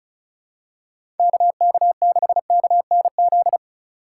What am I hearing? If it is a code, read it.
Morse code, heard as a keyed tone, about 34 wpm, 700 Hz. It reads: KK6KNZ
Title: USA Call Signs - 6 Characters - 3x Character Spacing - 35wpm